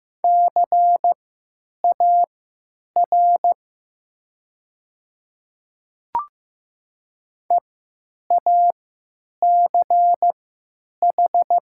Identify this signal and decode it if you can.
Morse code, heard as a keyed tone, about 15 wpm, spaced as for 9 wpm, 700 Hz. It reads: CAR EACH
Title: 3x Character Spacing - Top 300 Words - 15wpm